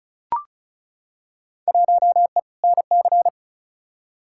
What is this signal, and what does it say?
Morse code, heard as a keyed tone, about 34 wpm, 700 Hz. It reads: 1E NC
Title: Field Day Exchanges - 35wpm